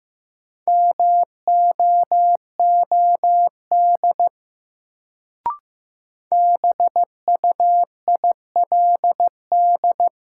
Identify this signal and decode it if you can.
Morse code, heard as a keyed tone, about 15 wpm, 700 Hz. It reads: MOOD BUILD
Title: Sentences from Top 1100 Words - New Word Drill Alt - 15wpm